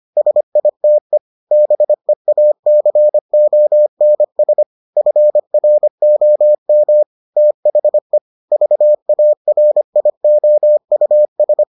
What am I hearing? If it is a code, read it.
Morse code, heard as a keyed tone, about 25 wpm, 600 Hz. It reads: SITE BEACONS FROM THE VARIOUS